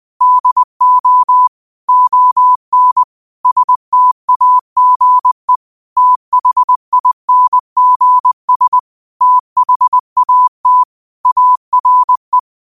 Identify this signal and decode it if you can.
Morse code, heard as a keyed tone, about 20 wpm, 1000 Hz. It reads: DO ON STAGE THINGS THAT ARE